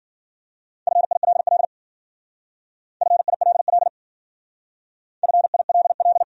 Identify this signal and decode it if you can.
Morse code, heard as a keyed tone, about 52 wpm, 700 Hz. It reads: FILL FILL FILL